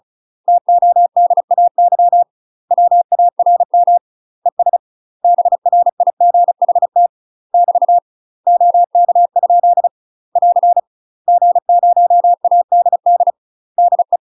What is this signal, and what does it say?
Morse code, heard as a keyed tone, about 35 wpm, 700 Hz. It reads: TODAY WARM ES BRIGHT <BT> OK? <AR> G0ADD DE
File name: UK MASTER QSOs TRAINING / 35WPM / UK MASTER QSOs 0015